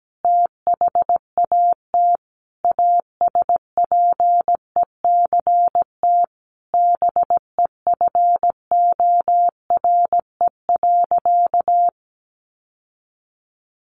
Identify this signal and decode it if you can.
Morse code, heard as a keyed tone, about 17 wpm, 700 Hz. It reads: THAT ASPECT BEFORE.
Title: Sentences from Top 1300 Words 17wpm